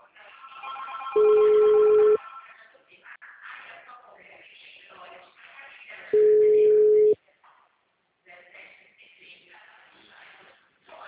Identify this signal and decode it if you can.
Ringback tone (Europe, 425 Hz), with speech.
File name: out- -2045- - - .18058